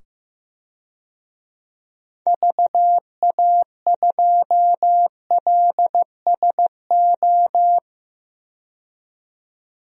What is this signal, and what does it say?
Morse code, heard as a keyed tone, about 15 wpm, 700 Hz. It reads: VA2LSO
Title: DXCC-100 Call Signs - 6 Characters - 15wpm